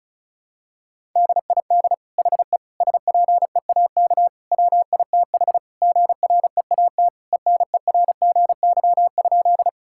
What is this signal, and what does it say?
Morse code, heard as a keyed tone, about 36 wpm, 700 Hz. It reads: DID HE SPEAK WITH GREAT ENERGY?
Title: Sentences from Top 1000 Words - Review Part 6 - 35wpm